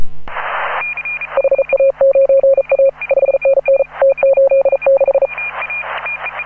Morse code, heard as a keyed tone, about 37 wpm, 540 Hz. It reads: HA9A 5NN T86
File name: AUDIO2018 / HA9A CW